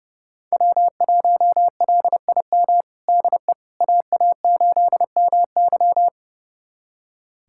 Morse code, heard as a keyed tone, about 30 wpm, 700 Hz. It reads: W1LIM DE AA8MY
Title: USA Callsign DE Callsign - 5 Characters - 30wpm